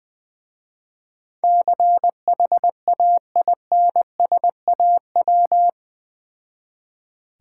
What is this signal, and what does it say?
Morse code, heard as a keyed tone, about 19 wpm, 700 Hz. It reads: CHAINSAW
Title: Difficult Words with Hints - 20wpm